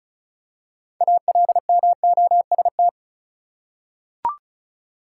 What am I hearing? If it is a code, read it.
Morse code, heard as a keyed tone, about 36 wpm, 700 Hz. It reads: ALMOST
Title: Six Letter Words - 35wpm